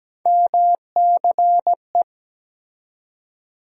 Morse code, heard as a keyed tone, about 17 wpm, 700 Hz. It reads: MCE
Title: USA Call Sign Postfixes - 3 Letters 17wpm